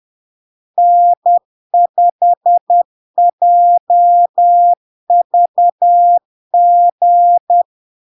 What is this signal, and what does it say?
Morse code, heard as a keyed tone, about 10 wpm, 700 Hz. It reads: N5JVG
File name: LARK-FD-2022-Log-in-CW-10wpm-700-Hz